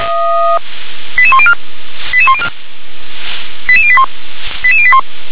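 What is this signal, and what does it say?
Five-tone selective call (ZVEI), heard as quick runs of five tones, tones of about 70 ms. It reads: 88104 88104 88081 88081
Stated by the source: ZVEI five-tone selective call received with an SDR.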